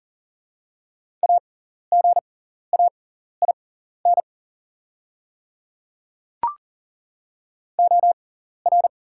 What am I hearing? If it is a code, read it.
Morse code, heard as a keyed tone, about 40 wpm, spaced as for 14 wpm, 700 Hz. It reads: AGAIN OR
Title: 6x Character Spacing - Top 300 Words - 40wpm